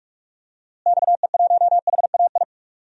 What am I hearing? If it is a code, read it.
Morse code, heard as a keyed tone, about 42 wpm, 700 Hz. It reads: XE1HAI
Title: DXCC-100 Call Signs - 6 Characters - Repeated 3x - 45wpm